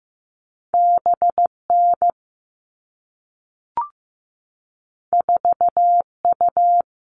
Morse code, heard as a keyed tone, about 15 wpm, 700 Hz. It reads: BN 4U
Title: All Cut Numbers - 2-Digit 15wpm